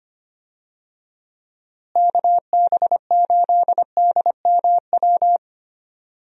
Morse code, heard as a keyed tone, about 25 wpm, 700 Hz. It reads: KB8DMW
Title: USA Call Signs - 6 Characters - Repeated 2x - 25wpm